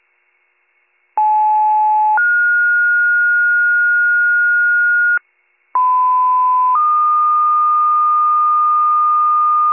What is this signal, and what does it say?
Two-tone page, heard as a one-second tone, then a three-second tone, about 860 then 1430 Hz, about 1010 then 1220 Hz, with noise. It is heard from 1.2 to 9.7 s.